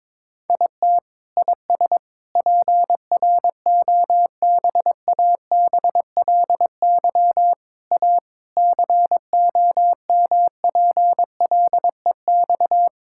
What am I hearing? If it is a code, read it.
Morse code, heard as a keyed tone, about 22 wpm, 700 Hz. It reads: IT IS PROBABLY A COMPLEX
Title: Sentences from Top 1000 Words - Review Part 6 - 22wpm